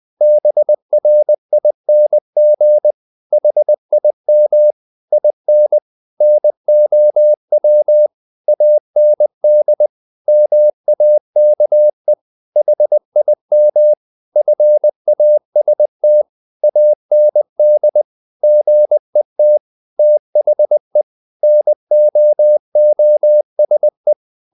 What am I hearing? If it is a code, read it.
Morse code, heard as a keyed tone, about 20 wpm, 600 Hz. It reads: BRING HIM IN NOW AND MAKE HIM FAST AND GET THE NOOSE
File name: the old man and the sea part 2 20wpm